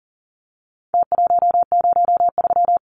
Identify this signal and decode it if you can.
Morse code, heard as a keyed tone, about 40 wpm, 700 Hz. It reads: T103
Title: Numbers - Contest Style - 4-Digit 40wpm